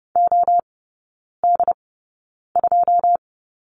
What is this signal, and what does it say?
Morse code, heard as a keyed tone, about 30 wpm, spaced as for 9 wpm, 700 Hz. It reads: OD2
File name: course / 30:10wpm / 087 Lesson 15e - Number 5 - Three Characters 30wpm